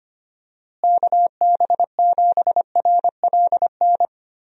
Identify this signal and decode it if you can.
Morse code, heard as a keyed tone, about 25 wpm, 700 Hz. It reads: KB7RLN
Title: USA Call Signs - 6 Characters - Repeated 2x - 25wpm